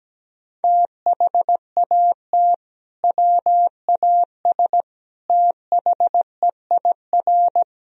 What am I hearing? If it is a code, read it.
Morse code, heard as a keyed tone, about 17 wpm, 700 Hz. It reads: THAT WAS THEIR
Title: Sentences from Top 1000 Words - Review Part 2 - 17wpm